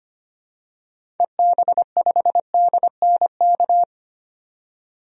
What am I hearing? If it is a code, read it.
Morse code, heard as a keyed tone, about 25 wpm, 700 Hz. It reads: EB5DNK
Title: DXCC-100 Call Signs - 6 Characters - 25wpm